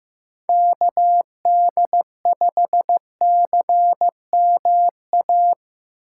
Morse code, heard as a keyed tone, about 15 wpm, 700 Hz. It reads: KD5CMA
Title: Speed Racing - USA 2x3 Call Signs 15wpm